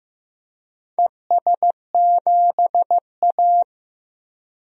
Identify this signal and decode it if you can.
Morse code, heard as a keyed tone, about 15 wpm, 700 Hz. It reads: ES7A
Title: DXCC-100 Call Signs - 4 Characters - Repeated 3x - 15wpm